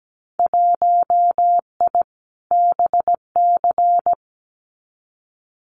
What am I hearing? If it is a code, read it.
Morse code, heard as a keyed tone, about 17 wpm, 700 Hz. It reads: 1I BC
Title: Winter Field Day Exchanges Encore 17wpm